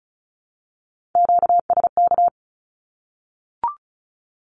Morse code, heard as a keyed tone, about 34 wpm, 700 Hz. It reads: QSK
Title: Q-Codes Repeated 3x 35wpm